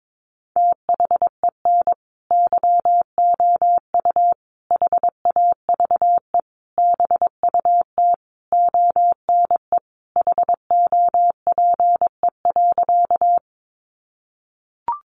Morse code, heard as a keyed tone, about 22 wpm, 700 Hz. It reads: THEN YOU HAVE BUT ONE HOPE.
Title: Sentences from Top 1000 Words 22wpm